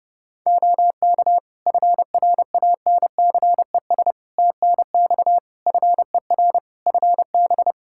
Morse code, heard as a keyed tone, about 28 wpm, 700 Hz. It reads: OK FRANCES TNX FER FB
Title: Practice QSO - Exchange 4 - Ending - 30wpm